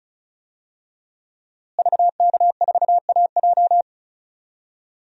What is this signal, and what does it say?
Morse code, heard as a keyed tone, about 36 wpm, 700 Hz. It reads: VK4AJ
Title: DXCC-100 Call Signs - 5 Characters - Repeated 2x - 35wpm